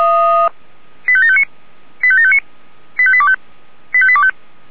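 Five-tone selective call (ZVEI), heard as quick runs of five tones, tones of about 80 ms. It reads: 85759 85759 85825 85825
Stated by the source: ZVEI five-tone call received on SDR.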